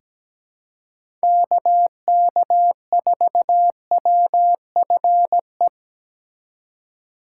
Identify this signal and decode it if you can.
Morse code, heard as a keyed tone, about 17 wpm, 700 Hz. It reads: KK4WFE
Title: USA Call Signs - 6 Characters - 2x Character Spacing - 17wpm